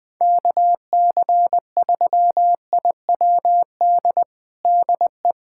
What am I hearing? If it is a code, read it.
Morse code, heard as a keyed tone, about 21 wpm, 700 Hz. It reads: KC3IWD DE
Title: USA Callsign DE Callsign - 6 Characters - 20wpm